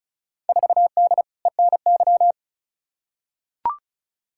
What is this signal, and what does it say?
Morse code, heard as a keyed tone, about 36 wpm, 700 Hz. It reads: 4D ENY
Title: Field Day Exchanges Encore - 35wpm